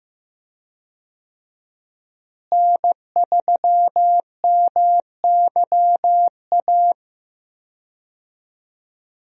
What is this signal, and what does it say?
Morse code, heard as a keyed tone, about 15 wpm, 700 Hz. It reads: N3MYA
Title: USA Call Signs - 5 Characters - Repeated 2x - 15wpm